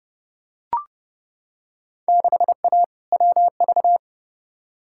Morse code, heard as a keyed tone, about 30 wpm, 700 Hz. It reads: 6A WV
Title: Field Day Exchanges - 30wpm